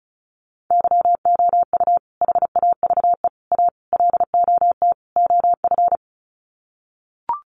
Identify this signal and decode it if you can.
Morse code, heard as a keyed tone, about 34 wpm, 700 Hz. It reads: YOU HAVE A LOT OF
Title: Sets of 5 Words Encore 35wpm